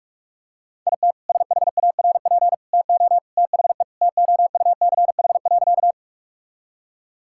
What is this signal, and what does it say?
Morse code, heard as a keyed tone, about 41 wpm, 700 Hz. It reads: IT SHARP TO THE TOUCH.
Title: Sentences from Top 1600 Words 45wpm